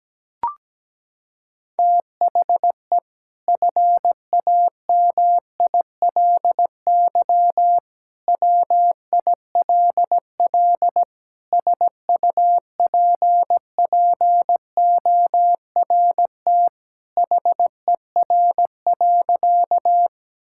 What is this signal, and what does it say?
Morse code, heard as a keyed tone, about 17 wpm, 700 Hz. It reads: THE FAMILY WILL SUPPORT HER.
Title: Sentences from Top 1000 Words - Review Part 5 - 17wpm